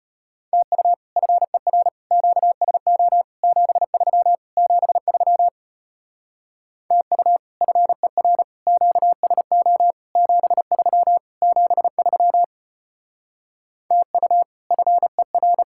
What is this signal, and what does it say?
Morse code, heard as a keyed tone, about 33 wpm, 700 Hz. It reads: TU FER QSO 73 73 TU FER QSO 73 73 TU FER